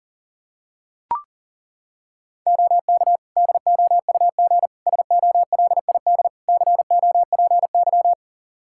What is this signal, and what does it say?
Morse code, heard as a keyed tone, about 37 wpm, 700 Hz. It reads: OK DOUG SOLID COPY
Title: Practice QSO - Exchange 3 - More Info - 40wpm